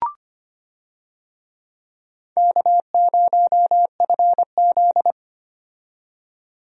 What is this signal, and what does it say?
Morse code, heard as a keyed tone, about 25 wpm, 700 Hz. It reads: K0FZ
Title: USA Call Signs - 4 Characters - 25wpm